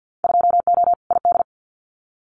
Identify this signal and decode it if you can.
Morse code, heard as a keyed tone, about 50 wpm, 700 Hz. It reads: 2O ID
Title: Winter Field Day Exchanges 50wpm